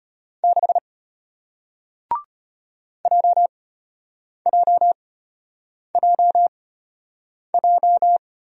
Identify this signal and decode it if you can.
Morse code, heard as a keyed tone, about 34 wpm, spaced as for 8 wpm, 700 Hz. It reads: 6 JJJJ